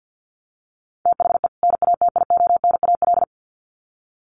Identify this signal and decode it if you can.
Morse code, heard as a keyed tone, about 50 wpm, 700 Hz. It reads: THE NATIONAL
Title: Sets of 2 Words 50wpm